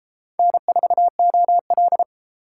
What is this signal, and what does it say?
Morse code, heard as a keyed tone, about 31 wpm, 700 Hz. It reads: N4OL